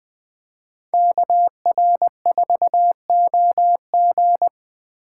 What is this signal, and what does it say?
Morse code, heard as a keyed tone, about 20 wpm, 700 Hz. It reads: KR4OG